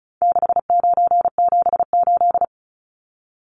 Morse code, heard as a keyed tone, about 34 wpm, 700 Hz. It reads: 6978
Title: Numbers - 4-Digit 35wpm